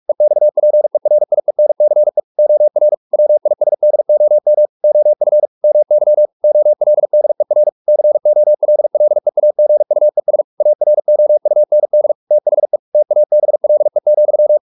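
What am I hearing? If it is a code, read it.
Morse code, heard as a keyed tone, about 44 wpm, 600 Hz. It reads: EXPERIENCE OR WISDOM OF MY OLDER COLLEAGUES AROUND THE TABLE,
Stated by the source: Morse practice at 45 wpm, the novel Never by Ken Follett in English, from CW med Gnister.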